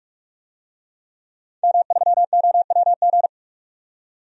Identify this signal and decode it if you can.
Morse code, heard as a keyed tone, about 43 wpm, 700 Hz. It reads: M3OWG
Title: DXCC-100 Call Signs - 5 Characters - 45wpm